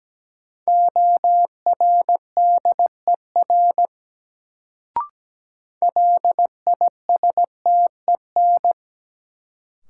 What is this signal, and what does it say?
Morse code, heard as a keyed tone, about 17 wpm, 700 Hz. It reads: ORDER LISTEN
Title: Sentences from Top 400 Words - New Word Drill - 17wpm